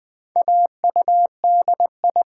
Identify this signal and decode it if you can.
Morse code, heard as a keyed tone, about 20 wpm, 700 Hz. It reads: AUDI
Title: Build A Word - 1 Letter At A Time - 5-Letter Words - 20wpm